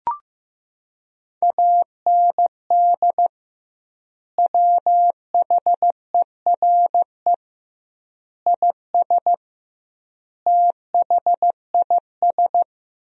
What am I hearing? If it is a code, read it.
Morse code, heard as a keyed tone, about 15 wpm, 700 Hz. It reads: AND WHERE IS THIS
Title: 2x Word-Spacing - Sentences from Top 200 Words - 15wpm